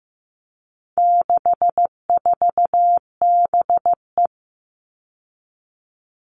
Morse code, heard as a keyed tone, about 15 wpm, 700 Hz. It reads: 64BE
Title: All Cut Numbers - 4-Digit 15wpm